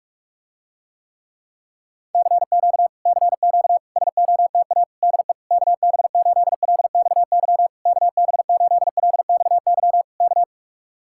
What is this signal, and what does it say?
Morse code, heard as a keyed tone, about 45 wpm, 700 Hz. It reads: CQ CQ SOTA DE KB8LXY KB8LXY K